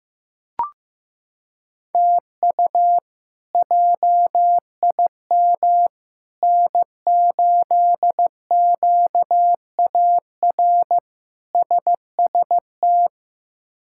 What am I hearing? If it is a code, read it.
Morse code, heard as a keyed tone, about 16 wpm, 700 Hz. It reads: TU JIM N8QAR SST
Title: K1USN SST QSO Practice - 15wpm